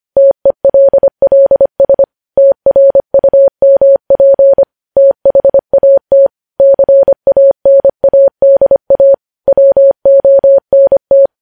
Morse code, heard as a keyed tone, about 25 wpm, 550 Hz. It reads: TELLS TRUMP THAT CANADA WONT
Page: news headlines in morse code at 25 wpm